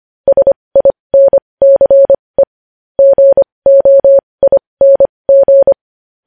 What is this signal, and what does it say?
Morse code, heard as a keyed tone, about 25 wpm, spaced as for 20 wpm, 550 Hz. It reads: SINCE GOING
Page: News Headlines in Morse Code at 20 WPM